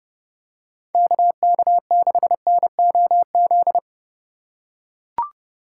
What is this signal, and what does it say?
Morse code, heard as a keyed tone, about 30 wpm, 700 Hz. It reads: KK6NOZ